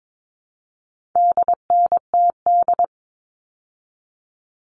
Morse code, heard as a keyed tone, about 22 wpm, 700 Hz. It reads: DNTD
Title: All Cut Numbers - 4-Digit 22wpm